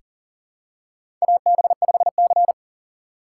Morse code, heard as a keyed tone, about 40 wpm, 690 Hz. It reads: AB5C
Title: K1USN SST QSO Practice - 40wpm